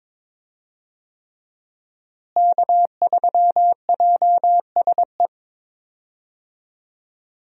Morse code, heard as a keyed tone, about 21 wpm, 700 Hz. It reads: K3JSE
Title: USA Call Signs - 5 Characters - Repeated 3x - 22wpm